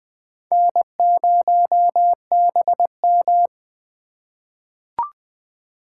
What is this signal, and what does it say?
Morse code, heard as a keyed tone, about 20 wpm, 700 Hz. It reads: N0BM
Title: USA Call Signs - 4 Characters - 3x Character Spacing - 20wpm